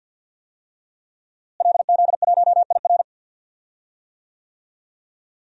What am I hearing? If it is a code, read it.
Morse code, heard as a keyed tone, about 51 wpm, 700 Hz. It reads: LZ1IR
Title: DXCC-100 Call Signs - 5 Characters - Repeated 3x - 50wpm